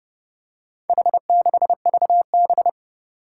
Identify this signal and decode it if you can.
Morse code, heard as a keyed tone, about 30 wpm, 690 Hz. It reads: H6VB
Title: Four Challenging Letter-Number Combinations 30wpm